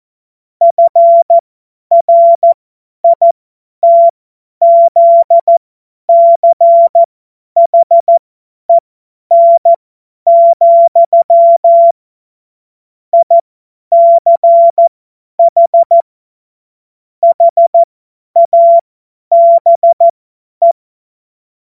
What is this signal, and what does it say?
Morse code, heard as a keyed tone, about 14 wpm, spaced as for 10 wpm, 680 Hz. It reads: FRITZCHEN, ICH HABE
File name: Morsekurs / Witze / Witz 28. September 2021, 50 F70